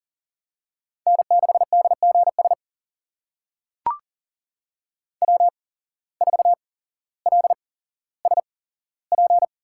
Morse code, heard as a keyed tone, about 42 wpm, spaced as for 17 wpm, 700 Hz. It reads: N6DGS W4LSP
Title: USA Call Signs - 5 Characters - 8x Character Spacing - 40wpm